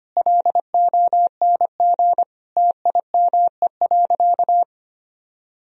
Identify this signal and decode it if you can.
Morse code, heard as a keyed tone, about 25 wpm, 700 Hz. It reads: LONG TIME.
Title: Sentences from Top 1000 Words - Review Part 3 - 25wpm